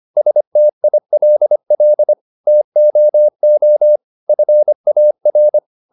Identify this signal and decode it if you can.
Morse code, heard as a keyed tone, about 25 wpm, 600 Hz. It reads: STILL TOO FAR